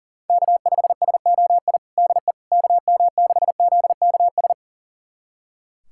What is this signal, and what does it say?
Morse code, heard as a keyed tone, about 40 wpm, 700 Hz. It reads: K5SOI DE KM6ZKS
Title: USA Callsign DE Callsign 40wpm